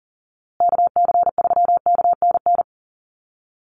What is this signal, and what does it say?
Morse code, heard as a keyed tone, about 40 wpm, 700 Hz. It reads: KC3KNN
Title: USA Call Signs Repeated 3x 40wpm